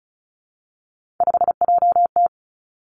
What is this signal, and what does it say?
Morse code, heard as a keyed tone, about 36 wpm, 700 Hz. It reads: 5JT
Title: Three Challenging Letter-Number Combinations 35wpm